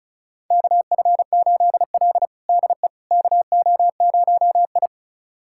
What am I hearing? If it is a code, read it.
Morse code, heard as a keyed tone, about 36 wpm, 700 Hz. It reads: KF8L DE KO0I
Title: USA Callsign DE Callsign - 4 Characters - 35wpm